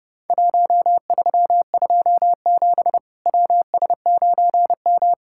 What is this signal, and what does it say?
Morse code, heard as a keyed tone, about 29 wpm, 700 Hz. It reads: 1327 WS9M
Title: Worked All Europe DX Contest - QTC Exchange - 30wpm